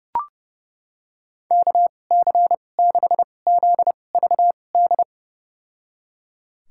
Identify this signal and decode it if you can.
Morse code, heard as a keyed tone, about 30 wpm, spaced as for 22 wpm, 700 Hz. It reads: KC6ZVD
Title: USA Call Signs - 6 Characters - 2x Character Spacing - 30wpm